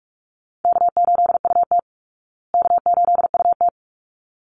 Morse code, heard as a keyed tone, about 44 wpm, 700 Hz. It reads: K8UT K8UT
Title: Easy Call Signs Repeated 2x - 45wpm